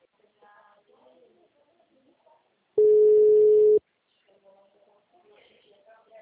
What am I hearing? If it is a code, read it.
Ringback tone (Europe, 425 Hz), with speech.